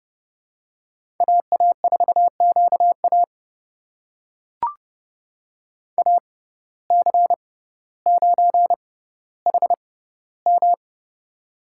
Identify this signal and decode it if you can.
Morse code, heard as a keyed tone, about 30 wpm, spaced as for 15 wpm, 700 Hz. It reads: AA4QA AC9HM
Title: USA Call Signs - 5 Characters - 6x Character Spacing - 30wpm